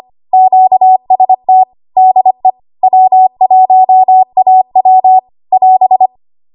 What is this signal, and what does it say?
Morse code, heard as a keyed tone, about 25 wpm, 750 Hz. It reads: QST DE W1AW <AS>
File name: Archive / 25 WPM / 25WPM